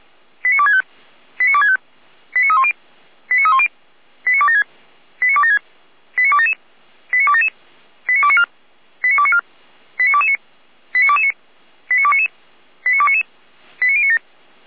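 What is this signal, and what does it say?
Five-tone selective call (ZVEI), heard as quick runs of five tones, tones of about 70 ms. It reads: 89265 89265 89310 89310 89276 89276 89277 89277 89294 89294 89229 89229 89299 89299 89097
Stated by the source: ZVEI five-tone call received on an SDR, from the Feuerwehr Orlamünde (fire brigade) recording server.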